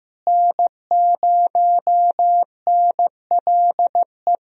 Morse code, heard as a keyed tone, about 16 wpm, 700 Hz. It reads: N0NLE
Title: USA Call Signs - 5 Characters - Repeated 3x - 15wpm